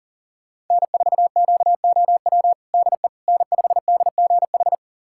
Morse code, heard as a keyed tone, about 40 wpm, 700 Hz. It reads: N4QOW DE N5DGH
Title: USA Callsign DE Callsign - 5 Characters - 40wpm